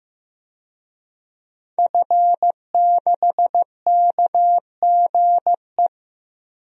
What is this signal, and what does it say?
Morse code, heard as a keyed tone, about 15 wpm, 700 Hz. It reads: F6KGE
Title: DXCC-100 Call Signs - 5 Characters - Repeated 2x - 15wpm